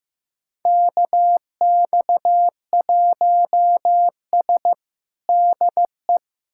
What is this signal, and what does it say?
Morse code, heard as a keyed tone, about 15 wpm, 700 Hz. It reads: KX1S DE
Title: USA Callsign DE Callsign - 4 Characters - 15wpm